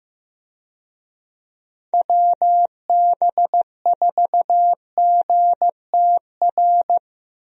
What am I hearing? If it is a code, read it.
Morse code, heard as a keyed tone, about 16 wpm, 700 Hz. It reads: WB4GTR
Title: USA Call Signs - 6 Characters - Repeated 3x - 15wpm